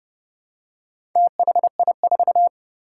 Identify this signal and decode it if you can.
Morse code, heard as a keyed tone, about 28 wpm, 700 Hz. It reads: THI4
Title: Four Challenging Letter-Number Combinations 30wpm